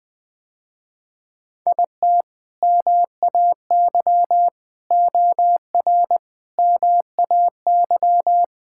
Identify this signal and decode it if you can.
Morse code, heard as a keyed tone, about 20 wpm, 700 Hz. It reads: IT MAY OR MAY
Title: Sentences with 5-Word Phrases - Spoken Beginning - 20wpm